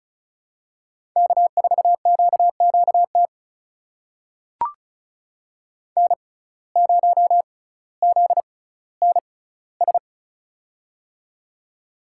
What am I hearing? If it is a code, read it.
Morse code, heard as a keyed tone, about 34 wpm, spaced as for 18 wpm, 700 Hz. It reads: K4QQT N0ZNS
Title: USA Call Signs - 5 Characters - 6x Character Spacing - 35wpm